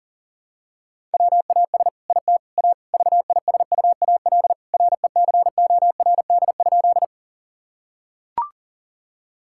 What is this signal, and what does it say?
Morse code, heard as a keyed tone, about 40 wpm, 700 Hz. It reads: WAS IT A VISUAL RECORD?
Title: Sentences from Top 1000 Words - Review Part 6 - 40wpm